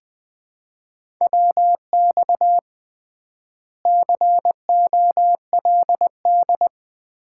Morse code, heard as a keyed tone, about 20 wpm, 700 Hz. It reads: WX COLD
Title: Common QSO Phrases - 3x Word-Spacing - 20wpm